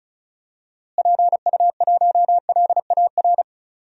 Morse code, heard as a keyed tone, about 36 wpm, 700 Hz. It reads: PU1LAR
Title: DXCC-100 Call Signs - 6 Characters - Repeated 3x - 35wpm